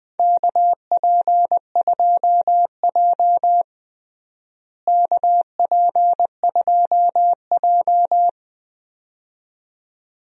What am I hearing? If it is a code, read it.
Morse code, heard as a keyed tone, about 20 wpm, 700 Hz. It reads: KP2J KP2J
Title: DXCC-100 Call Signs - 4 Characters - Repeated 2x - 20wpm